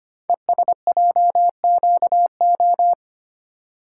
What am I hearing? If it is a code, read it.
Morse code, heard as a keyed tone, about 25 wpm, 700 Hz. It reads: ESJQO
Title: Code Groups - 5 Letters - 25wpm